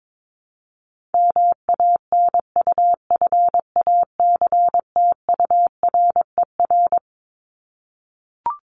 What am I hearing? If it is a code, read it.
Morse code, heard as a keyed tone, about 22 wpm, 700 Hz. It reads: MANUFACTURER
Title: Top 1751-2000 Words 22wpm